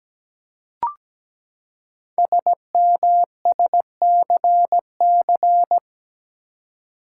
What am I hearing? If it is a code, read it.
Morse code, heard as a keyed tone, about 17 wpm, 700 Hz. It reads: SMSCC
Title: Code Groups - 5 Letters - 17wpm